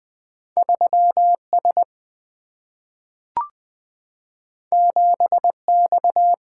Morse code, heard as a keyed tone, about 20 wpm, 700 Hz. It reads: 3 S 7 X
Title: Single-Letter Single-Number 20wpm